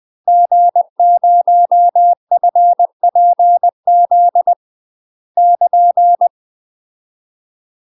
Morse code, heard as a keyed tone, about 20 wpm, 700 Hz. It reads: G0FPZ <KN>
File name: UK MASTER QSOs TRAINING / 20WPM / UK MASTER QSOs 0016